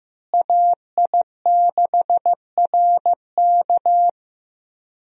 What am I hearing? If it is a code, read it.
Morse code, heard as a keyed tone, about 15 wpm, 700 Hz. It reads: AI6RK
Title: USA Call Signs - 5 Characters - Repeated 3x - 15wpm